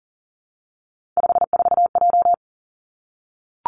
Morse code, heard as a keyed tone, about 38 wpm, 700 Hz. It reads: 54J
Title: Three Challenging Letter-Number Combinations 40wpm